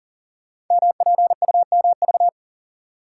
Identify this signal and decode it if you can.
Morse code, heard as a keyed tone, about 42 wpm, 700 Hz. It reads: MPUMV
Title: Code Groups - 5 Letters - 40wpm